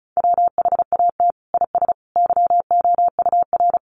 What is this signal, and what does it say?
Morse code, heard as a keyed tone, about 34 wpm, 700 Hz. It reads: WHAT IS YOUR
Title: Sentences from Top 600 Words 35wpm